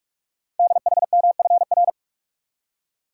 Morse code, heard as a keyed tone, about 44 wpm, 690 Hz. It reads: DHMFR
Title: Code Groups - 5 Letters - 45wpm